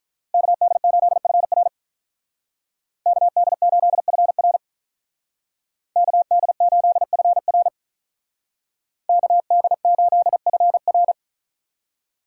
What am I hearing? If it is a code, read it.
Morse code, heard as a keyed tone, about 43 wpm, 700 Hz. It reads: KD8FR KD8FR KD8FR KD8FR
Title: Speed Racing - USA 2x2 Call Signs 35wpm